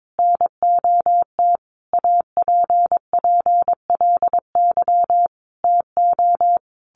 Morse code, heard as a keyed tone, about 22 wpm, 700 Hz. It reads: NOT APPLY TO
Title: Sentences from Top 1200 Words 22wpm